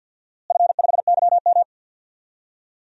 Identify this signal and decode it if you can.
Morse code, heard as a keyed tone, about 53 wpm, 700 Hz. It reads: F5YM
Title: DXCC-100 Call Signs - 4 Characters - Repeated 2x - 50wpm